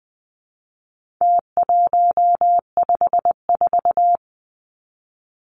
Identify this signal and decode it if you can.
Morse code, heard as a keyed tone, about 20 wpm, 700 Hz. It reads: T154
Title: Numbers - Contest Style - 4-Digit 20wpm